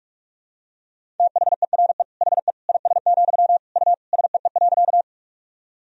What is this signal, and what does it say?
Morse code, heard as a keyed tone, about 46 wpm, 700 Hz. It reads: THERE HE IS, U SEE.
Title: Sentences from Top 100 Words and Abbreviations 45wpm